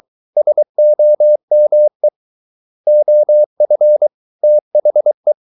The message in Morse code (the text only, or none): SOME OF THE